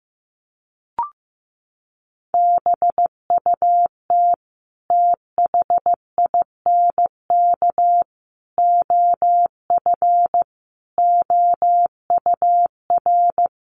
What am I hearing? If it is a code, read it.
Morse code, heard as a keyed tone, about 15 wpm, 700 Hz. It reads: BUT THINK OF OUR
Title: Sentences from Top 1100 Words 15wpm